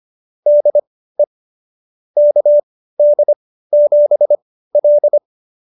Morse code, heard as a keyed tone, about 25 wpm, spaced as for 15 wpm, 590 Hz. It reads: DE KD7L
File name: qso222 15